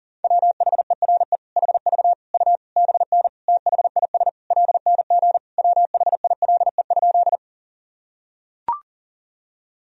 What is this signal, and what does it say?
Morse code, heard as a keyed tone, about 40 wpm, 700 Hz. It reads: WHERE HV U BN THIS LNG WHILE?
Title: Sentences from Top 300 Words and Abbreviations 40wpm